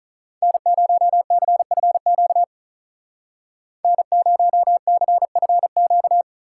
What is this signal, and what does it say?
Morse code, heard as a keyed tone, about 44 wpm, 700 Hz. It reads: N0CFQ N0CFQ